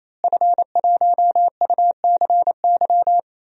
Morse code, heard as a keyed tone, about 28 wpm, 700 Hz. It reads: F1UCY
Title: DXCC-100 Call Signs - 5 Characters - Repeated 3x - 28wpm